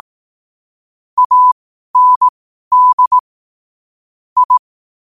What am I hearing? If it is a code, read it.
Morse code, heard as a keyed tone, about 18 wpm, spaced as for 12 wpm, 1000 Hz. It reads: AND I
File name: qotdfiles / 12wpm